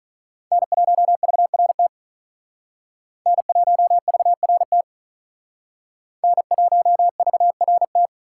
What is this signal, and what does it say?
Morse code, heard as a keyed tone, about 40 wpm, 700 Hz. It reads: N1VRT N1VRT N1VRT
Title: Speed Racing - USA 1x3 Call Signs 35wpm